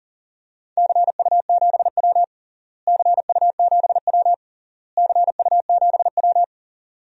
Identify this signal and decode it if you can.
Morse code, heard as a keyed tone, about 40 wpm, 700 Hz. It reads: CU7W CU7W CU7W
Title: DXCC-100 Call Signs - 4 Characters - Repeated 3x - 40wpm